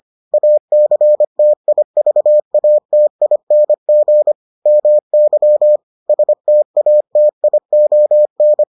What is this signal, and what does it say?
Morse code, heard as a keyed tone, about 25 wpm, 600 Hz. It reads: ACTIVATING MY STATION